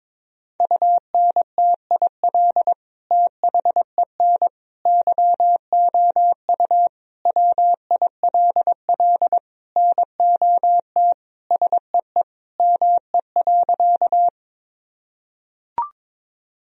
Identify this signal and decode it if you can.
Morse code, heard as a keyed tone, about 22 wpm, 700 Hz. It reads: UNTIL THEN YOU WILL NOT SEE ME.
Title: Sentences from Top 1000 Words - Review Part 1 - 22wpm